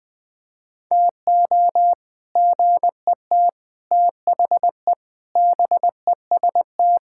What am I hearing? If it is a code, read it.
Morse code, heard as a keyed tone, about 21 wpm, 700 Hz. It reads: TO GET THE BEST
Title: Sentences with 3-Word Phrases - 20wpm